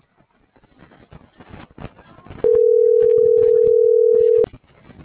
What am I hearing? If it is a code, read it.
Ringback tone (North America), 440 + 480 Hz.